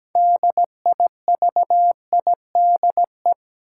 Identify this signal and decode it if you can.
Morse code, heard as a keyed tone, about 17 wpm, 700 Hz. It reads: DIVIDE
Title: Sentences from Top 1600 Words - New Word Drill Alt - 17wpm